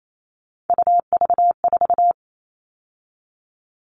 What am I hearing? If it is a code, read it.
Morse code, heard as a keyed tone, about 28 wpm, 700 Hz. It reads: UV4
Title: All Cut Numbers - 3-Digit 28wpm